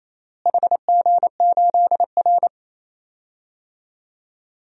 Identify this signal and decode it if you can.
Morse code, heard as a keyed tone, about 28 wpm, 700 Hz. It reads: HG8R
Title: DXCC-100 Call Signs - 4 Characters - Repeated 3x - 28wpm